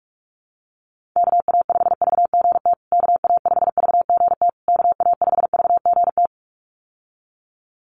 Morse code, heard as a keyed tone, about 44 wpm, 700 Hz. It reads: KA5VGT KA5VGT KA5VGT
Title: USA Call Signs Repeated 3x 45wpm